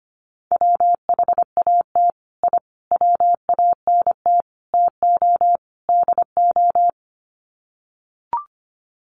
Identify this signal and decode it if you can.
Morse code, heard as a keyed tone, about 25 wpm, 700 Hz. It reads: WHAT I WANT TO DO
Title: Sets of 5 Words Encore 25wpm